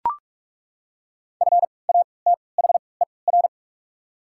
Morse code, heard as a keyed tone, about 42 wpm, spaced as for 25 wpm, 700 Hz. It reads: FATHER
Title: 3x Character Spacing - Top 500 Words - 45wpm